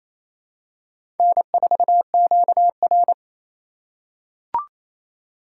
Morse code, heard as a keyed tone, about 28 wpm, 700 Hz. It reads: N4QR
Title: USA Call Signs - 4 Characters - 4x Character Spacing - 28wpm